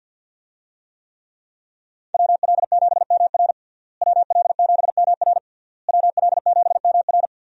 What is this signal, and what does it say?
Morse code, heard as a keyed tone, about 52 wpm, 700 Hz. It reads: WL7MR WL7MR WL7MR
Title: USA Call Signs - 5 Characters - Repeated 3x - 50wpm